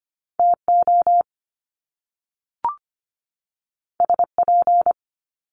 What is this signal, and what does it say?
Morse code, heard as a keyed tone, about 25 wpm, 700 Hz. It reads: TO SP
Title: Top Two Letter Combinations in Words 25wpm